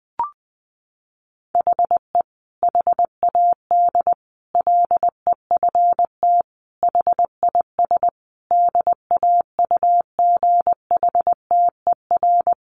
Morse code, heard as a keyed tone, about 20 wpm, 700 Hz. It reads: HE HAD LEFT HIS DAUGHTER
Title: Sentences from Top 1500 Words 20wpm